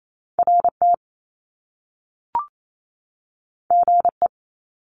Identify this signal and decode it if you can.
Morse code, heard as a keyed tone, about 28 wpm, 700 Hz. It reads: RT GE
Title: Top Two Letter Combinations in Words 28wpm